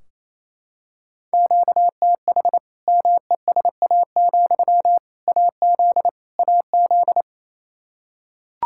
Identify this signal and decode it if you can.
Morse code, heard as a keyed tone, about 28 wpm, 700 Hz. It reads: QTH MESA, AZ AZ
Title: Practice QSO - Exchange 1 - Beginning - 28wpm